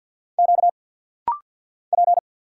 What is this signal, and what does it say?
Morse code, heard as a keyed tone, about 50 wpm, 690 Hz. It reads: Q P
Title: Single Letters - Rapid-Fire 50wpm